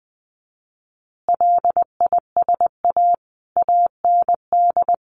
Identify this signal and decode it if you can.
Morse code, heard as a keyed tone, about 20 wpm, 700 Hz. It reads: LISA AND
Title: Additional Easy Phrases 20wpm